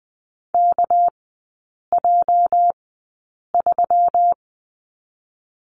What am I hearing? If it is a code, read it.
Morse code, heard as a keyed tone, about 20 wpm, spaced as for 8 wpm, 700 Hz. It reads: KJ3
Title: 218 Lesson 35e - Number 0 - Three Characters 20wpm